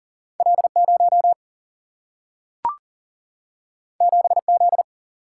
Morse code, heard as a keyed tone, about 40 wpm, 700 Hz. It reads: L 0 7 Z
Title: Single-Letter Single-Number 40wpm